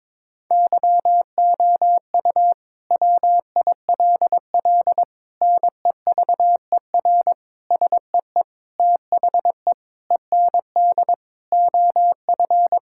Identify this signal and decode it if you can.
Morse code, heard as a keyed tone, about 22 wpm, 700 Hz. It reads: YOU WILL NEVER SEE THE END OF
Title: Sentences from Top 1600 Words 22wpm